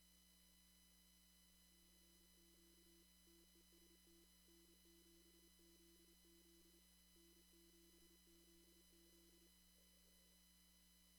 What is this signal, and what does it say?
Morse code, heard as a keyed tone, about 27 wpm, 340 Hz. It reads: CQ TEST N7ZZ N7ZZ
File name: CW / kt5j wpx cw-op-k5pi